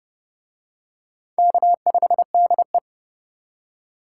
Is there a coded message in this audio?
Yes: Morse code from 1.4 to 2.8 s, heard as a keyed tone, about 29 wpm, 700 Hz. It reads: K5DE